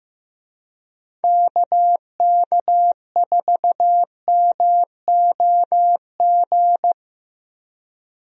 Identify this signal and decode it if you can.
Morse code, heard as a keyed tone, about 15 wpm, 700 Hz. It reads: KK4MOG
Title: USA Call Signs - 6 Characters - Repeated 3x - 15wpm